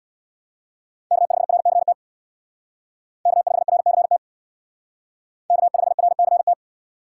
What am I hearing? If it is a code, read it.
Morse code, heard as a keyed tone, about 65 wpm, 700 Hz. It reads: K5RCT K5RCT K5RCT